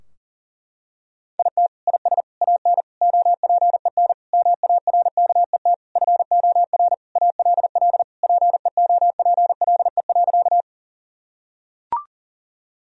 Morse code, heard as a keyed tone, about 40 wpm, 700 Hz. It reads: IT IS AN OPEN MARKET FOR ALL PEOPLE.
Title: Sentences from Top 1000 Words - Review Part 3 - 40wpm